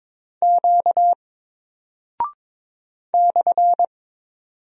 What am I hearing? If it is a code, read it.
Morse code, heard as a keyed tone, about 22 wpm, 700 Hz. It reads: Q /
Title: Single Character (Letter:Number:Punctuation:Prosign) 22wpm